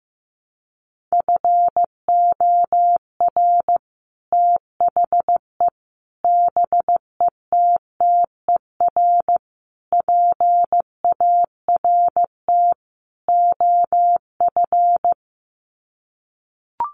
Morse code, heard as a keyed tone, about 14 wpm, 700 Hz. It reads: FOR THE BETTER PART OF
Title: Sets of 5 Words Encore2 15wpm